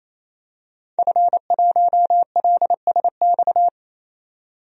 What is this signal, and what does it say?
Morse code, heard as a keyed tone, about 28 wpm, 700 Hz. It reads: F1LSX
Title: DXCC-100 Call Signs - 5 Characters - 28wpm